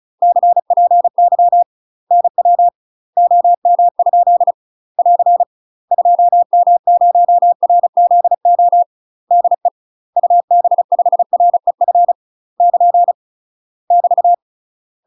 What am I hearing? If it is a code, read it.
Morse code, heard as a keyed tone, about 35 wpm, 700 Hz. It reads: CPY NW OM? <AR> 2M0RZO DE UB5REF <KN> <BT>
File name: UK MASTER QSOs TRAINING / 35WPM / UK MASTER QSOs 0038